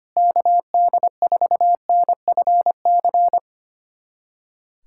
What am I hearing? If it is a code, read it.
Morse code, heard as a keyed tone, about 25 wpm, 700 Hz. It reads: KD4NFC
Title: USA Callsign DE Callsign - 6 Characters - 25wpm